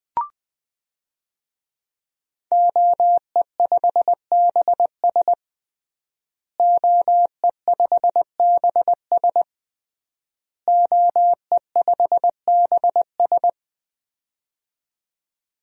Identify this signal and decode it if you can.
Morse code, heard as a keyed tone, about 20 wpm, 700 Hz. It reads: OE5BS OE5BS OE5BS
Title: DXCC-100 Call Signs - 5 Characters - Repeated 3x - 20wpm